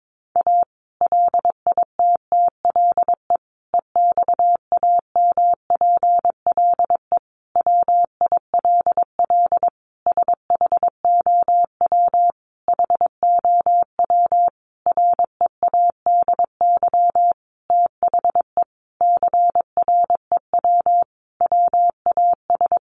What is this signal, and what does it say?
Morse code, heard as a keyed tone, about 22 wpm, 700 Hz. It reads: A LITTLE EXAMPLE WILL SHOW HOW READY THE CREW WAS
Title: Sentences from Top 1500 Words 22wpm